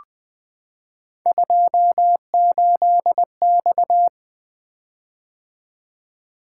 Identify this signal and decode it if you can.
Morse code, heard as a keyed tone, about 20 wpm, 700 Hz. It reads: 28X
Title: Three Letter-Number Combinations - 20wpm